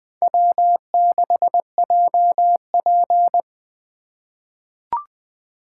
Morse code, heard as a keyed tone, about 21 wpm, 700 Hz. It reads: W6JP
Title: 2023 SOTA Chaser Call Signs - 20wpm